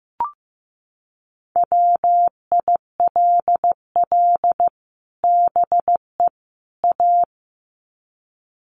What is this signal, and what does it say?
Morse code, heard as a keyed tone, about 15 wpm, 700 Hz. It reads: WILL BE A
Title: Sets of 3 Words 15wpm